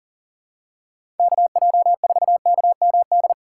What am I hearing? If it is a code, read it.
Morse code, heard as a keyed tone, about 40 wpm, 700 Hz. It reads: KJ4KMD
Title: USA Call Signs - 6 Characters - 2x Character Spacing - 40wpm